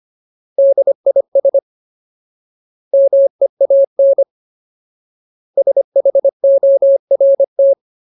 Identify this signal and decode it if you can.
Morse code, heard as a keyed tone, about 25 wpm, 550 Hz. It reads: DIS MEAN SHORT
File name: Ultimate Word List / 25WPM Ultimate